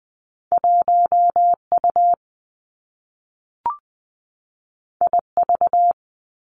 Morse code, heard as a keyed tone, about 20 wpm, 700 Hz. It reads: 1 U I V
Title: Two Challenging Letter-Number Combinations 20wpm